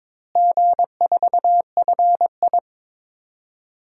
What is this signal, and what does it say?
Morse code, heard as a keyed tone, about 22 wpm, 700 Hz. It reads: G4FI
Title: DXCC-100 Call Signs - 4 Characters - Repeated 3x - 22wpm